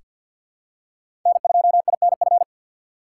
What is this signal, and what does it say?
Morse code, heard as a keyed tone, about 48 wpm, 700 Hz. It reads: N2INF
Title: USA Call Signs - 5 Characters - 2x Character Spacing - 50wpm